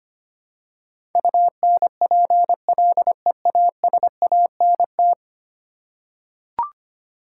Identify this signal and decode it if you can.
Morse code, heard as a keyed tone, about 25 wpm, 700 Hz. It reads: UNPLEASANT
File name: 10 Letter Words - First 3 Letters Given - 25wpm